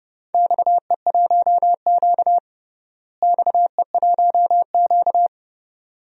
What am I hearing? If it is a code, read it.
Morse code, heard as a keyed tone, about 30 wpm, 700 Hz. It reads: XE1Q XE1Q
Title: DXCC-100 Call Signs - 4 Characters - Repeated 2x - 30wpm